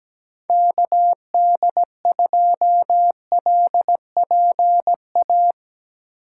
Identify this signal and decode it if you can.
Morse code, heard as a keyed tone, about 17 wpm, 700 Hz. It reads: KD2LPA
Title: USA Call Signs - 6 Characters - 2x Character Spacing - 17wpm